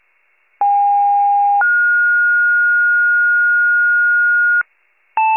Two-tone page, heard as a one-second tone, then a three-second tone, about 810 then 1430 Hz, with noise. It is heard from 0.6 to 4.6 s.
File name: wcalert